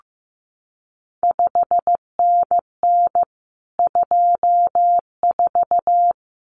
Morse code, heard as a keyed tone, about 15 wpm, 700 Hz. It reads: 5NN 24
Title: CQ WW DX Contest Exchange Encore - 15wpm